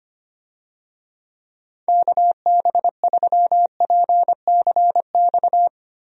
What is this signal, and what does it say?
Morse code, heard as a keyed tone, about 25 wpm, 700 Hz. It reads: KB3PCX
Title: USA Call Signs - 6 Characters - Repeated 3x - 25wpm